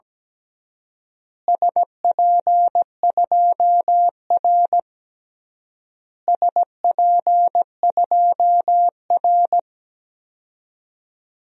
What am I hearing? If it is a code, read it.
Morse code, heard as a keyed tone, about 17 wpm, 700 Hz. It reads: SP2R SP2R
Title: DXCC-100 Call Signs - 4 Characters - Repeated 3x - 17wpm